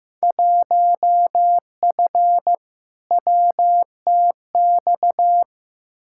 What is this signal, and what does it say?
Morse code, heard as a keyed tone, about 15 wpm, 700 Hz. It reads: 1F WTX
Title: Field Day Exchanges - 15wpm